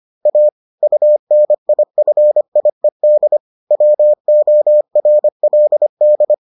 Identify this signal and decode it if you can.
Morse code, heard as a keyed tone, about 25 wpm, 600 Hz. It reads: A UNIFIED WORLD